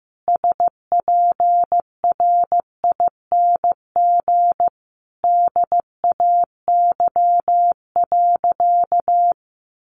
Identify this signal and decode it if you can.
Morse code, heard as a keyed tone, about 14 wpm, 700 Hz. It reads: SPRING DAY.
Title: Sentences from Top 1200 Words 15wpm